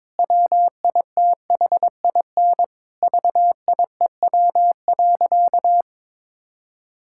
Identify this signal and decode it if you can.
Morse code, heard as a keyed tone, about 22 wpm, 700 Hz. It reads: WITHIN VIEW.
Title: Sentences from Top 1000 Words - Review Part 6 - 22wpm